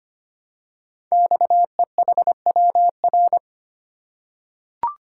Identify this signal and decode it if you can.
Morse code, heard as a keyed tone, about 25 wpm, 700 Hz. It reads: XEHWR
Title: Code Groups - 5 Letters - 25wpm